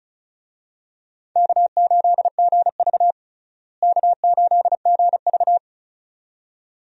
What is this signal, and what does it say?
Morse code, heard as a keyed tone, about 36 wpm, 700 Hz. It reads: K8GV K8GV
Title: USA Call Signs - 4 Characters - Repeated 2x - 35wpm